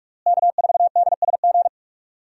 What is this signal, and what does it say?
Morse code, heard as a keyed tone, about 45 wpm, 700 Hz. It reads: K4DSG
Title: USA Call Signs - 5 Characters - Repeated 2x - 45wpm